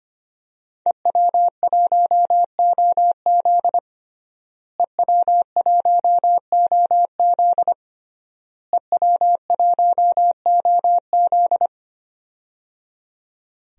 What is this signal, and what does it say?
Morse code, heard as a keyed tone, about 25 wpm, 700 Hz. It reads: EW1OZ EW1OZ EW1OZ
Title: DXCC-100 Call Signs - 5 Characters - Repeated 3x - 25wpm